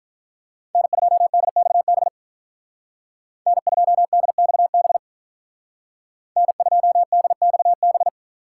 Morse code, heard as a keyed tone, about 43 wpm, 700 Hz. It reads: N2DXB N2DXB N2DXB
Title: Speed Racing - USA 1x3 Call Signs 35wpm